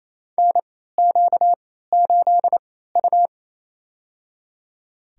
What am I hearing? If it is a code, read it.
Morse code, heard as a keyed tone, about 27 wpm, spaced as for 16 wpm, 700 Hz. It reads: NQ8U
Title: USA Call Signs - 3x Character Spacing - 28wpm